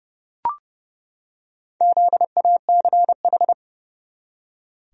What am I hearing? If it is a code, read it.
Morse code, heard as a keyed tone, about 30 wpm, 700 Hz. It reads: ZACH
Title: USA Common Names - Max 4 Letters - 30wpm